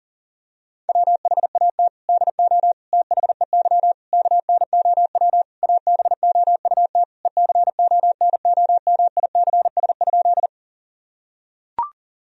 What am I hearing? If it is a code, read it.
Morse code, heard as a keyed tone, about 40 wpm, 700 Hz. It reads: WHAT DO THEY KNOW ABOUT ECONOMICS?
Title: Sentences from Top 1000 Words - Review Part 3 - 40wpm